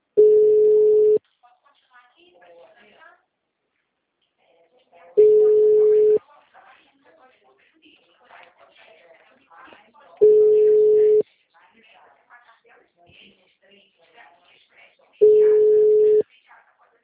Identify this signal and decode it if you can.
Ringback tone (Europe, 425 Hz), with speech.